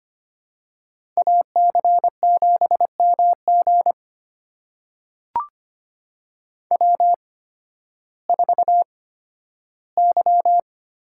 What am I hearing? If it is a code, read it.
Morse code, heard as a keyed tone, about 25 wpm, spaced as for 13 wpm, 700 Hz. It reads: AC7MG W4Y
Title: USA Call Signs - 8x Character Spacing - 25wpm